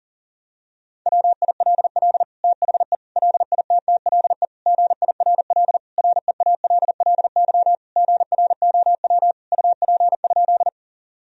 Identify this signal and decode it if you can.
Morse code, heard as a keyed tone, about 40 wpm, 700 Hz. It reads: WILL THE LITTLE GIRL REALLY GROW UP?
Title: Sentences from Top 1000 Words - Review Part 1 - 40wpm